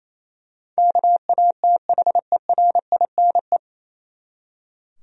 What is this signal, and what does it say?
Morse code, heard as a keyed tone, about 28 wpm, 700 Hz. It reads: KATHERINE
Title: USA Common Names - 28wpm